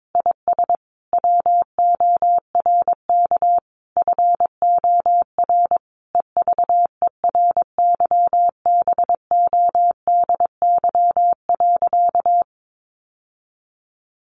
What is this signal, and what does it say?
Morse code, heard as a keyed tone, about 22 wpm, 700 Hz. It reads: IS WORK FOR EVERYBODY.
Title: Sentences from Top 1400 Words 22wpm